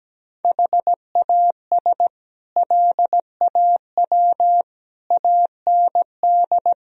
Morse code, heard as a keyed tone, about 17 wpm, 700 Hz. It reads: HAS LAW AND
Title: Sentences from Top 1000 Words - Review Part 3 - 17wpm